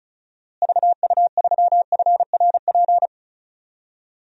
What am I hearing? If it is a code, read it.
Morse code, heard as a keyed tone, about 36 wpm, 700 Hz. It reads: VU3FRP
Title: DXCC-100 Call Signs - 6 Characters - Repeated 3x - 35wpm